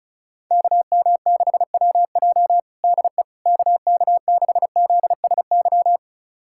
Morse code, heard as a keyed tone, about 36 wpm, 690 Hz. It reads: KM6WJ DE KK6ZSY
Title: USA Callsign DE Callsign 35wpm